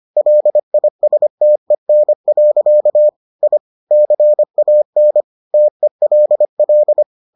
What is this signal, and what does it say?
Morse code, heard as a keyed tone, about 25 wpm, 600 Hz. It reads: LISTEN. I CAN TELL